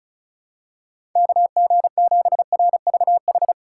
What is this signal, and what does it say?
Morse code, heard as a keyed tone, about 36 wpm, 700 Hz. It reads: KG7RVH
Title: USA Call Signs - 6 Characters - 4x Character Spacing - 35wpm